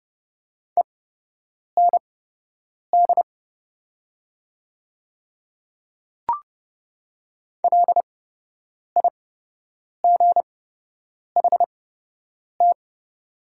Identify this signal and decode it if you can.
Morse code, heard as a keyed tone, about 30 wpm, spaced as for 8 wpm, 700 Hz. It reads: END LIGHT
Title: 8x Character Spacing - Top 200 Words - 30wpm